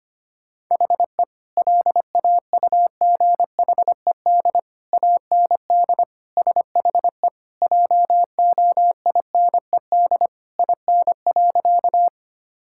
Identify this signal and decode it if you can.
Morse code, heard as a keyed tone, about 25 wpm, 700 Hz. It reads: HE LAUGHED AND SHE JOINED IN.
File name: practice / 25wpm / Sentences with 2-Word Phrases - 25wpm